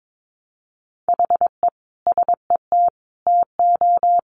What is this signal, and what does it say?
Morse code, heard as a keyed tone, about 22 wpm, 700 Hz. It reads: HE SET TO
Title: Sentences from Top 1200 Words 22wpm